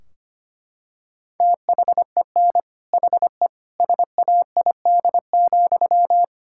Morse code, heard as a keyed tone, about 25 wpm, 700 Hz. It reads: THEN HE SAID,
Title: Sentences from Top 1600 Words 25wpm